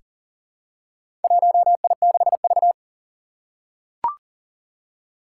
Morse code, heard as a keyed tone, about 40 wpm, 700 Hz. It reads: 1I6V
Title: Four Challenging Letter-Number Combinations 40wpm